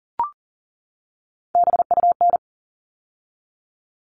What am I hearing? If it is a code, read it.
Morse code, heard as a keyed tone, about 40 wpm, 700 Hz. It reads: BUN
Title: All Cut Numbers - 3-Digit 40wpm